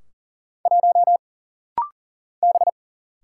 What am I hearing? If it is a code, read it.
Morse code, heard as a keyed tone, about 40 wpm, 700 Hz. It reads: 1 B
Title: Single Letter-Number - Rapid-Fire 40wpm